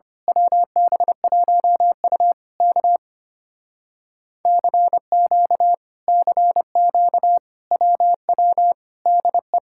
Morse code, heard as a keyed tone, about 25 wpm, 700 Hz. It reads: WB1U K CQ CQ WW DE
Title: Speed-Racing Common QSO Phrases - 25wpm